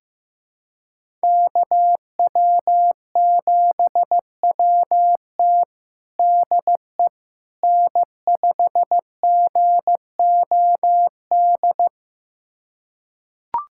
Morse code, heard as a keyed tone, about 15 wpm, 700 Hz. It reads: KW7WT DE N5GOD
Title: USA Callsign DE Callsign - 5 Characters - 15wpm